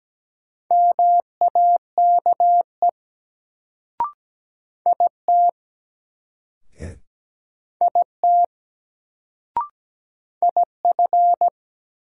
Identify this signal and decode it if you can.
Morse code, heard as a keyed tone, about 17 wpm, 700 Hz. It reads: MAKE IT IT IF
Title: Sentences from Top 100 Words - New Word Drill - 17wpm